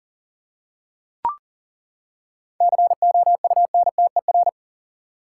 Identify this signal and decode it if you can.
Morse code, heard as a keyed tone, about 37 wpm, 700 Hz. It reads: COUNTER
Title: Seven Letter Words - 40wpm